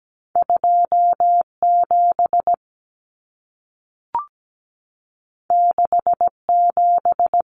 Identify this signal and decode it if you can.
Morse code, heard as a keyed tone, about 17 wpm, 700 Hz. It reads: 27 67
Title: Common Cut Numbers - 2-Digit 17wpm